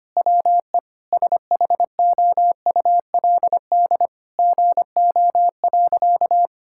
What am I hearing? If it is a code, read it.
Morse code, heard as a keyed tone, about 25 wpm, 700 Hz. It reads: WE SHOULD GO.
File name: practice / 25wpm / Sentences with 4-Word Phrases - 25wpm